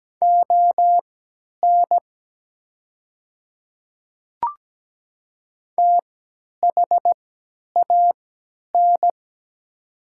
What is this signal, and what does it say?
Morse code, heard as a keyed tone, about 17 wpm, spaced as for 10 wpm, 700 Hz. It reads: ON THAN
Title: 3x Character Spacing - Top 100 Words - 17wpm